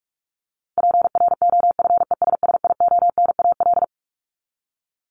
Morse code, heard as a keyed tone, about 44 wpm, 700 Hz. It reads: PROFESSIONAL
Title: Top 501-750 Words 45wpm